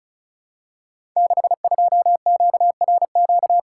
Morse code, heard as a keyed tone, about 36 wpm, 700 Hz. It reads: 62QRQ
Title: Code Groups - 5 Letter:Numbers - 35wpm